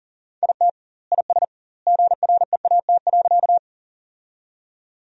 Morse code, heard as a keyed tone, about 40 wpm, 700 Hz. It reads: IT IS GREAT.
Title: 2x Word-Spacing - Sentences from Top 200 Words - 40wpm